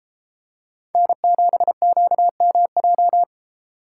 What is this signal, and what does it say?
Morse code, heard as a keyed tone, about 33 wpm, 700 Hz. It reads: N7QMJ